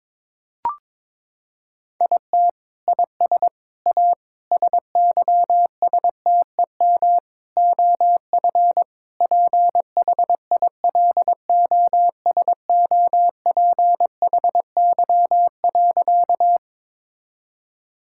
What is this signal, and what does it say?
Morse code, heard as a keyed tone, about 21 wpm, 700 Hz. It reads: IT IS A SYSTEM OF PHILOSOPHY.
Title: Sentences from Top 1000 Words - Review Part 6 - 22wpm